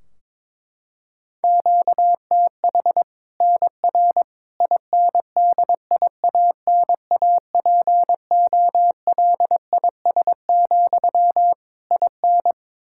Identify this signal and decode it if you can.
Morse code, heard as a keyed tone, about 22 wpm, 700 Hz. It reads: QTH NR INDIANAPOLIS, IN
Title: Practice QSO - Exchange 1 - Beginning - 22wpm